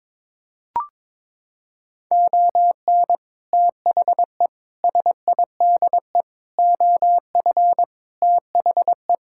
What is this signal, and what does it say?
Morse code, heard as a keyed tone, about 22 wpm, 700 Hz. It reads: ON THE SIDE OF THE
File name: practice / 22wpm / Sets of 6 Words - 22wpm